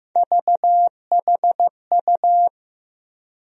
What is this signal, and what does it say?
Morse code, heard as a keyed tone, about 15 wpm, 700 Hz. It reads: VHU
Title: USA Call Sign Postfixes - 3 Letters 15wpm